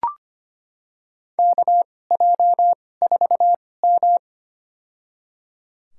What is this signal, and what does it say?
Morse code, heard as a keyed tone, about 25 wpm, spaced as for 18 wpm, 700 Hz. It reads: KJ4M
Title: USA Call Signs - 2x Character Spacing - 25wpm